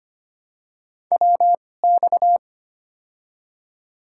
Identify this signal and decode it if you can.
Morse code, heard as a keyed tone, about 25 wpm, spaced as for 18 wpm, 700 Hz. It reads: WX
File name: Top 90 QSO Elements - 2x Character Spacing - 25wpm